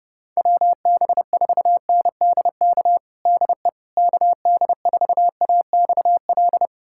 Morse code, heard as a keyed tone, about 32 wpm, 700 Hz. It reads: WB4NDK DE KD4AXL
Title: USA Callsign DE Callsign - 6 Characters - 30wpm